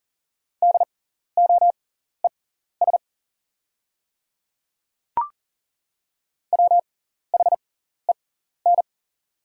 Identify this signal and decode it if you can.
Morse code, heard as a keyed tone, about 40 wpm, spaced as for 14 wpm, 690 Hz. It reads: DOES WHEN
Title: 6x Character Spacing - Top 200 Words - 40wpm